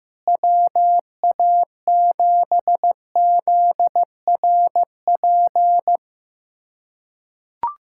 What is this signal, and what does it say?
Morse code, heard as a keyed tone, about 16 wpm, 700 Hz. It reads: WA7ZRP
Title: USA Callsign DE Callsign - 6 Characters - 15wpm